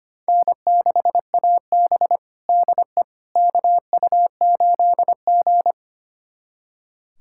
Morse code, heard as a keyed tone, about 25 wpm, 700 Hz. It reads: N6AB DE KU8G
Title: USA Callsign DE Callsign - 4 Characters - 25wpm